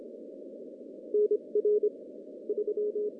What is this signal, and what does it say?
Morse code, heard as a keyed tone, about 27 wpm, spaced as for 23 wpm, 420 Hz. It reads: NR 3
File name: contests / CW-SS-k5tr- -1500